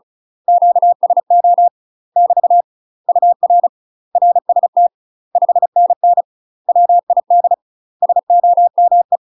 Morse code, heard as a keyed tone, about 36 wpm, 700 Hz. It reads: QSO <BT> UR RST 5NN WID SOME
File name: UK MASTER QSOs TRAINING / 35WPM / UK MASTER QSOs 0010